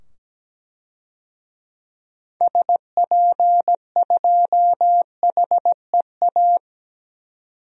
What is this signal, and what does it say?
Morse code, heard as a keyed tone, about 17 wpm, 700 Hz. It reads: SP2HEA
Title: DXCC-100 Call Signs - 6 Characters - 17wpm